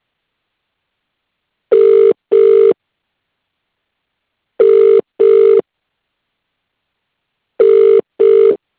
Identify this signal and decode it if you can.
Ringback tone (UK), 400 + 450 Hz.